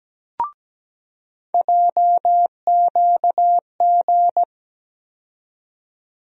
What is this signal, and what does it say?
Morse code, heard as a keyed tone, about 17 wpm, 700 Hz. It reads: JQG
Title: USA Call Sign Postfixes - 3 Letters 17wpm